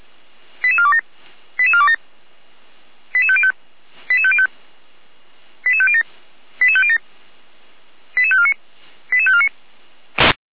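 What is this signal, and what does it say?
Five-tone selective call (ZVEI), heard as quick runs of five tones, tones of about 70 ms. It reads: 88427 88427 88505 88505 88507 88507 88549 88549
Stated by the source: ZVEI five-tone call received by SDR, from the server of feuerwehr-orlamuende.org (fire brigade Orlamünde).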